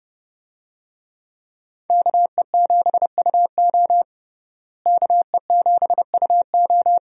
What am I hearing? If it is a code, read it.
Morse code, heard as a keyed tone, about 30 wpm, 700 Hz. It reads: KE7UO KE7UO
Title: USA Call Signs - 5 Characters - Repeated 3x - 30wpm